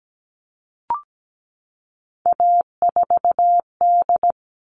Morse code, heard as a keyed tone, about 17 wpm, 700 Hz. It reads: A4D
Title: All Cut Numbers - 3-Digit 17wpm